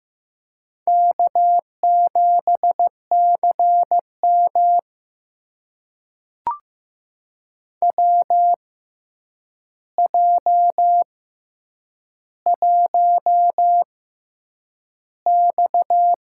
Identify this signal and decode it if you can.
Morse code, heard as a keyed tone, about 15 wpm, spaced as for 8 wpm, 700 Hz. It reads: K7CM WJ1X
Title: USA Call Signs - 4 Characters - 6x Character Spacing - 15wpm